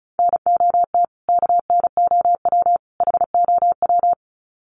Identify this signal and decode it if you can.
Morse code, heard as a keyed tone, about 34 wpm, 700 Hz. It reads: NOT KNOW HOW
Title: Sets of 3 Words 35wpm